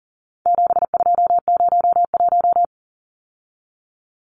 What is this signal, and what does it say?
Morse code, heard as a keyed tone, about 40 wpm, 700 Hz. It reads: 7201
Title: Numbers - 4-Digit 40wpm